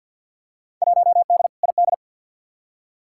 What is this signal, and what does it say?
Morse code, heard as a keyed tone, about 51 wpm, 700 Hz. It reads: 1D ID
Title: Field Day Exchanges - 50wpm